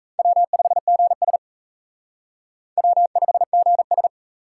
Morse code, heard as a keyed tone, about 44 wpm, 700 Hz. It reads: W5GS W5GS